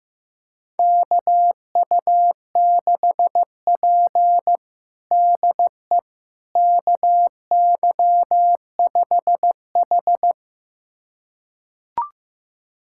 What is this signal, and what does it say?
Morse code, heard as a keyed tone, about 15 wpm, 700 Hz. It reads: KU6P DE KY5H
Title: USA Callsign DE Callsign - 4 Characters - 15wpm